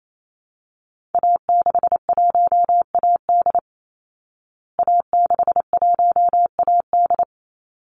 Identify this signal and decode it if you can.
Morse code, heard as a keyed tone, about 28 wpm, 700 Hz. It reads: A61AD A61AD
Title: Call Signs Repeated 3x - 28wpm